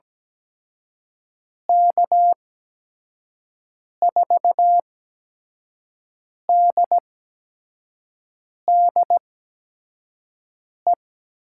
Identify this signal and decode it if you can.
Morse code, heard as a keyed tone, about 17 wpm, spaced as for 5 wpm, 700 Hz. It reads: K4DDE
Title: USA Call Signs - 8x Character Spacing - 17wpm